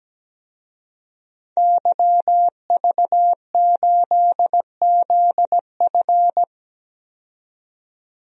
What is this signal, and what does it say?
Morse code, heard as a keyed tone, about 17 wpm, 700 Hz. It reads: YV8ZF
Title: DXCC-100 Call Signs - 5 Characters - 17wpm